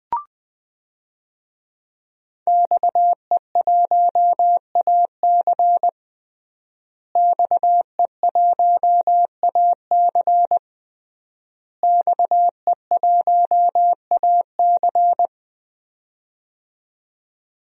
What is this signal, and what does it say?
Morse code, heard as a keyed tone, about 20 wpm, 700 Hz. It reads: XE1AC XE1AC XE1AC
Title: DXCC-100 Call Signs - 5 Characters - Repeated 3x - 20wpm